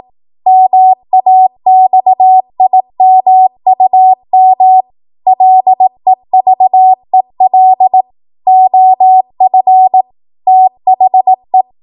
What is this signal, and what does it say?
Morse code, heard as a keyed tone, about 18 wpm, 750 Hz. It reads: MAXIMUM LEVEL OF THE
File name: Morse / 18WPM